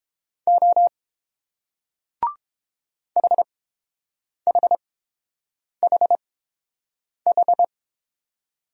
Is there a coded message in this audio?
Yes: Morse code from 0.5 to 7.6 s, heard as a keyed tone, about 30 wpm, spaced as for 7 wpm, 700 Hz. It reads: O HHHH